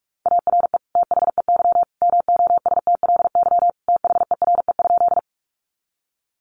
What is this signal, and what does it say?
Morse code, heard as a keyed tone, about 44 wpm, 700 Hz. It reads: ARE THEY MOSTLY THERE?
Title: Sentences from Top 1300 Words 45wpm